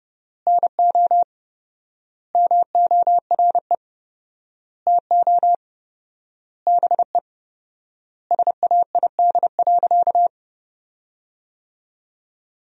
Morse code, heard as a keyed tone, about 30 wpm, 700 Hz. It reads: NO MORE TO BE SAID.
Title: 4x Word-Spacing - Sentences from Top 100 Words - 30wpm